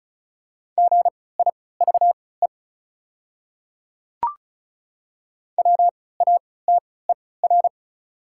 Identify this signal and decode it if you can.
Morse code, heard as a keyed tone, about 34 wpm, spaced as for 20 wpm, 700 Hz. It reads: GIVE WATER
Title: 3x Character Spacing - Top 300 Words - 35wpm